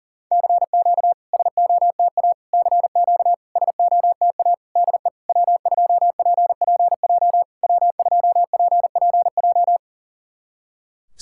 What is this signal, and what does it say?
Morse code, heard as a keyed tone, about 40 wpm, 700 Hz. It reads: CQ SOTA CQ SOTA DE W2PPJ W2PPJ
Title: SOTA QSO Practice 40wpm